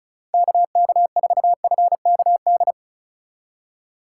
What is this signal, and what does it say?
Morse code, heard as a keyed tone, about 34 wpm, 700 Hz. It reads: KK4FKD
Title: USA Call Signs - 6 Characters - 4x Character Spacing - 35wpm